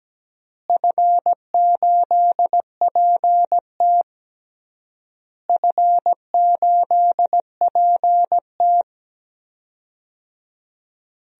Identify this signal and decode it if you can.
Morse code, heard as a keyed tone, about 17 wpm, 700 Hz. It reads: F8PT F8PT
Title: DXCC-100 Call Signs - 4 Characters - Repeated 3x - 17wpm